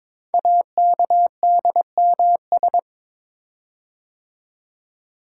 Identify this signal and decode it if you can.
Morse code, heard as a keyed tone, about 22 wpm, 700 Hz. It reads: AKDMS
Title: Code Groups - 5 Letters - 22wpm